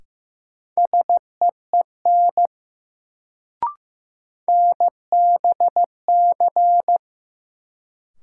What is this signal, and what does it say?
Morse code, heard as a keyed tone, about 15 wpm, 700 Hz. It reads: SEEN NBC
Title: Sets of 5 Words Encore - New Word Drill - 15wpm